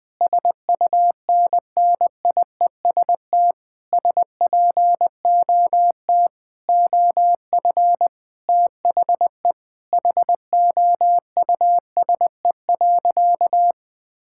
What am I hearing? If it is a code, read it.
Morse code, heard as a keyed tone, about 21 wpm, 700 Hz. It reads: SUNNIEST SPOT OF THE HOUSE.
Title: Difficult Sentences with Hints - 20wpm